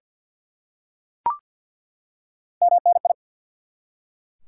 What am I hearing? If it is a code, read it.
Morse code, heard as a keyed tone, about 51 wpm, 700 Hz. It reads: MNI
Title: Top 90 QSO Elements 50wpm